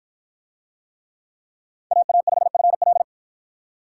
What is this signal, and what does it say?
Morse code, heard as a keyed tone, about 50 wpm, 700 Hz. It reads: AA5FL